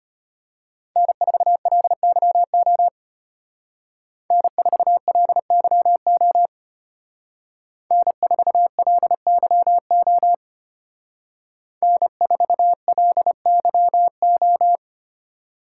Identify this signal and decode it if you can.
Morse code, heard as a keyed tone, about 31 wpm, 700 Hz. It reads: N4LYO N4LYO N4LYO N4LYO
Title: Speed Racing - USA 1x3 Call Signs 25wpm